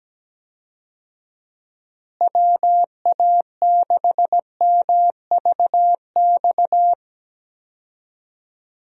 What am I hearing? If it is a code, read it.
Morse code, heard as a keyed tone, about 17 wpm, 700 Hz. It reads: WA6MVX
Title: USA Call Signs - 6 Characters - Repeated 2x - 17wpm